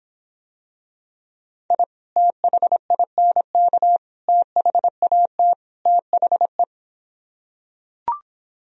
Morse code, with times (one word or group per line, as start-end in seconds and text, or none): 1.7-1.8 I
2.2-4.0 THINK
4.3-5.5 THAT
5.9-6.6 THE